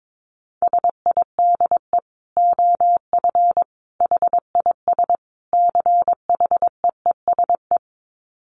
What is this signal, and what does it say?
Morse code, heard as a keyed tone, about 22 wpm, 700 Hz. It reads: SIDE OF HIS CHEESE
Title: Dit Challenge 22wpm